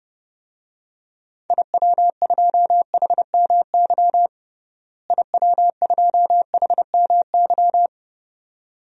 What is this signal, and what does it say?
Morse code, heard as a keyed tone, about 30 wpm, 700 Hz. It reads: IW2HMY IW2HMY
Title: DXCC-100 Call Signs - 6 Characters - Repeated 2x - 30wpm